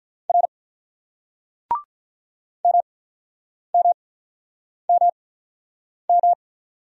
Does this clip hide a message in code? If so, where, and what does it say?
Yes: Morse code from 0.3 to 6.3 s, heard as a keyed tone, about 48 wpm, 700 Hz. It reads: R M M M M